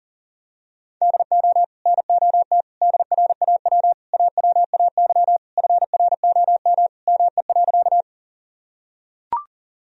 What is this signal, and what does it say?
Morse code, heard as a keyed tone, about 40 wpm, 700 Hz. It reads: DO NOT DRAW AWAY FROM ME.
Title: Sentences from Top 1000 Words - Review Part 1 - 40wpm